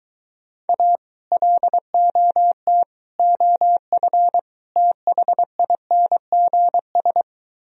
Morse code, heard as a keyed tone, about 23 wpm, 700 Hz. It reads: A LOT OF THINGS